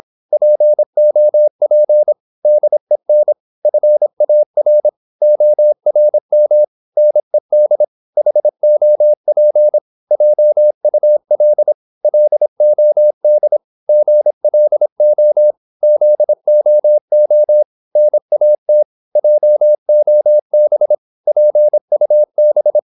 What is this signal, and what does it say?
Morse code, heard as a keyed tone, about 26 wpm, 600 Hz. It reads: POP DEN FAR ORM NED HOP JUL LOD GLO ZOO NAT JOB PUB
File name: Korte ord DK 26wpm